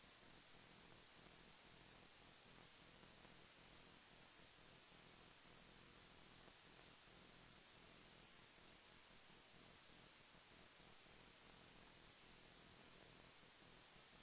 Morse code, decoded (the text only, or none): none